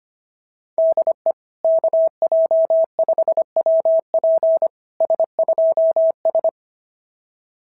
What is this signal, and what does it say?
Morse code, heard as a keyed tone, about 22 wpm, 650 Hz. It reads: DE KJ5WP S2S
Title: SOTA QSO Practice 25wpm